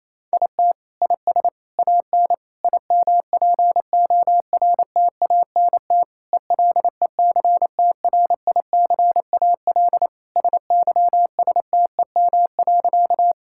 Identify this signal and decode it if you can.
Morse code, heard as a keyed tone, about 28 wpm, 700 Hz. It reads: IT IS AN IMPORTANT ELECTRICAL SYSTEM.
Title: Sentences from Top 1000 Words - Review Part 6 - 28wpm